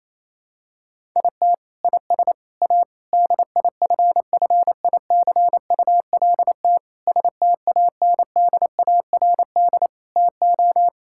Morse code, heard as a keyed tone, about 28 wpm, 700 Hz. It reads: IT IS A DIFFICULT STANDARD TO
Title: Sentences from Top 1000 Words - Review Part 6 - 28wpm